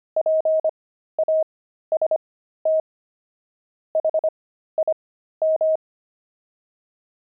Morse code, heard as a keyed tone, about 25 wpm, spaced as for 13 wpm, 630 Hz. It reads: PAST HIM
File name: pangram-fs-13-cs-25-axle